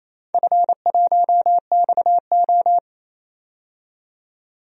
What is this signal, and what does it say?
Morse code, heard as a keyed tone, about 28 wpm, 700 Hz. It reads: F1XO
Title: DXCC-100 Call Signs - 4 Characters - 28wpm